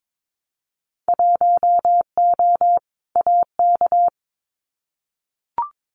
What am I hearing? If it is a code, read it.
Morse code, heard as a keyed tone, about 22 wpm, 700 Hz. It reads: 1O AK
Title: Winter Field Day Exchanges Encore 22wpm